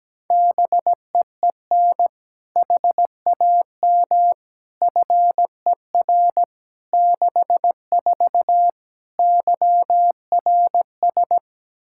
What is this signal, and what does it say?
Morse code, heard as a keyed tone, about 17 wpm, 700 Hz. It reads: BEEN HAM FER 64 YRS
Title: K1USN SST QSO Practice 17wpm